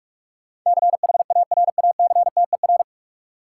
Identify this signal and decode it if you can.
Morse code, heard as a keyed tone, about 41 wpm, 700 Hz. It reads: CHARACTER
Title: Sentences from Top 1200 Words - New Word Drill - 45wpm